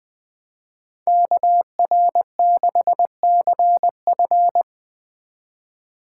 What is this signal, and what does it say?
Morse code, heard as a keyed tone, about 20 wpm, 700 Hz. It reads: KR6CF